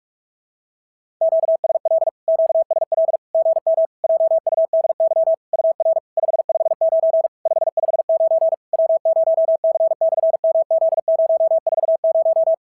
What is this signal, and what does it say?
Morse code, heard as a keyed tone, about 46 wpm, 650 Hz. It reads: QSL QSL GM JUDY UR 559 559 W0C/MZ040
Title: SOTA QSO Practice 45wpm